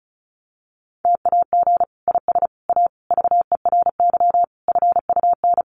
Morse code, heard as a keyed tone, about 34 wpm, 700 Hz. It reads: TAG IS A VERY FUN
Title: Additional Easy Phrases 35wpm